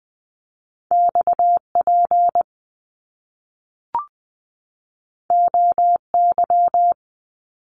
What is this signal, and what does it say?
Morse code, heard as a keyed tone, about 20 wpm, 700 Hz. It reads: XP OY
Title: Top Two Letter Combinations in Words 20wpm